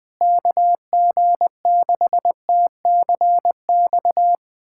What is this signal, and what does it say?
Morse code, heard as a keyed tone, about 20 wpm, 700 Hz. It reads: KG6TCX
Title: USA Callsign DE Callsign 20wpm